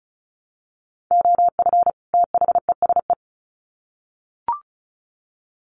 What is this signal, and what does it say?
Morse code, heard as a keyed tone, about 34 wpm, 700 Hz. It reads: OF THESE
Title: Sets of 2 Words 35wpm